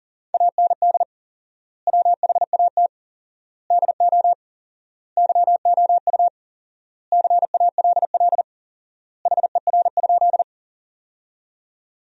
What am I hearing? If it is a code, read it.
Morse code, heard as a keyed tone, about 40 wpm, 700 Hz. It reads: AND WHAT DO YOU CALL HER?
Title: 4x Word-Spacing - Sentences from Top 100 Words - 40wpm